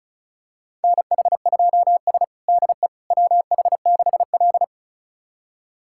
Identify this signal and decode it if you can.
Morse code, heard as a keyed tone, about 36 wpm, 700 Hz. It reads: NH2S DE WH6L
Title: USA Callsign DE Callsign - 4 Characters - 35wpm